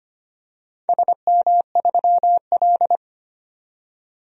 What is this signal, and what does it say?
Morse code, heard as a keyed tone, about 25 wpm, 700 Hz. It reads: SM3L
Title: DXCC-100 Call Signs - 4 Characters - 25wpm